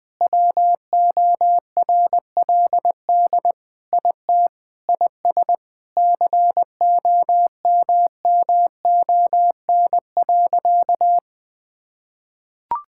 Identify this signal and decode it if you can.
Morse code, heard as a keyed tone, about 20 wpm, 700 Hz. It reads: WORLD IT IS COMMON.
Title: Sentences with 6-Word Phrases - 20wpm